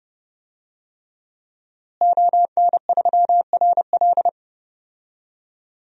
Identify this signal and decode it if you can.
Morse code, heard as a keyed tone, about 31 wpm, 700 Hz. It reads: ON3RL
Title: DXCC-100 Call Signs - 5 Characters - Repeated 3x - 30wpm